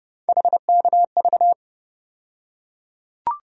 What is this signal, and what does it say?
Morse code, heard as a keyed tone, about 32 wpm, 700 Hz. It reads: HKV
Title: USA Call Sign Postfixes - 3 Letters 30wpm